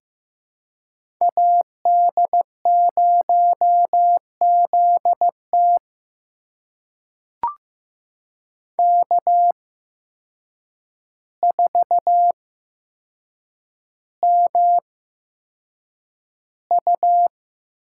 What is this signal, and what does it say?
Morse code, heard as a keyed tone, about 15 wpm, spaced as for 7 wpm, 700 Hz. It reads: AD0ZT K4MU
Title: USA Call Signs - 8x Character Spacing - 15wpm